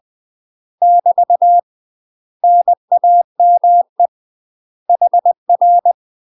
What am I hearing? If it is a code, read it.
Morse code, heard as a keyed tone, about 20 wpm, 700 Hz. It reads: <BT> NAME HR
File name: UK MASTER QSOs TRAINING / 20WPM / UK MASTER QSOs 0071